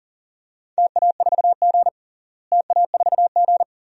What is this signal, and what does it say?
Morse code, heard as a keyed tone, about 40 wpm, 700 Hz. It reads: TA4G TA4G
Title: DXCC-100 Call Signs - 4 Characters - Repeated 2x - 40wpm